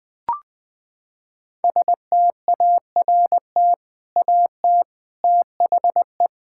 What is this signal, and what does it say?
Morse code, heard as a keyed tone, about 20 wpm, 700 Hz. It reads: START AT THE
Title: Sentences from Top 1000 Words - Review Part 6 - 20wpm